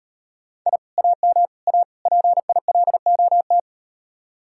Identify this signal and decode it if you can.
Morse code, heard as a keyed tone, about 38 wpm, 700 Hz. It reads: I AM A PILOT